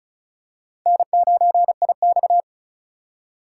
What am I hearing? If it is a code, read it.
Morse code, heard as a keyed tone, about 34 wpm, 700 Hz. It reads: N9IX
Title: USA Call Signs - 4 Characters - 2x Character Spacing - 35wpm